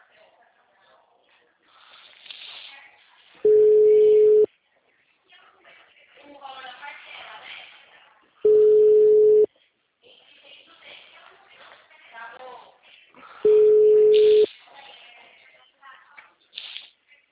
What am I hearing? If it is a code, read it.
Ringback tone (Europe, 425 Hz), with speech.
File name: out- -2016- - - .66984